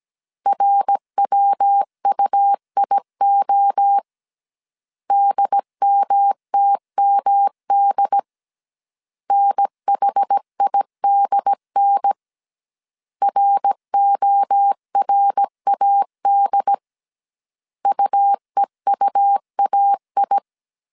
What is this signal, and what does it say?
Morse code, heard as a keyed tone, about 17 wpm, 800 Hz. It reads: RWUIO DMTMD NHIDN RORAD UEUAI